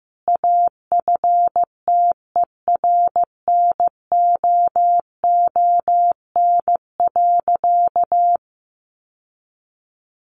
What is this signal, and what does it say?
Morse code, heard as a keyed tone, about 15 wpm, 700 Hz. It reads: AFTERNOON.
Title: Sentences from Top 1500 Words 15wpm